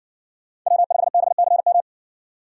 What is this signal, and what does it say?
Morse code, heard as a keyed tone, about 53 wpm, 700 Hz. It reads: W5BCM